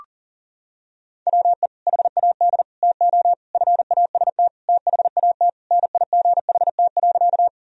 Morse code, heard as a keyed tone, about 40 wpm, 700 Hz. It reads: WE HAD TO FAST THAT NIGHT.
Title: Sentences from Top 1000 Words - Review Part 2 - 40wpm